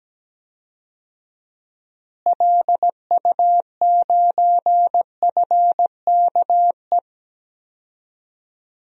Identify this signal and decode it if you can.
Morse code, heard as a keyed tone, about 17 wpm, 700 Hz. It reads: LU9FKE
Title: DXCC-100 Call Signs - 6 Characters - Repeated 3x - 17wpm